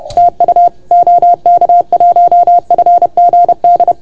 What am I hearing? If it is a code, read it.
Morse code, heard as a keyed tone, about 30 wpm, 690 Hz. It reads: TU OK1FGD
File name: Archiv / CUC381 / OK1FGD CUC 381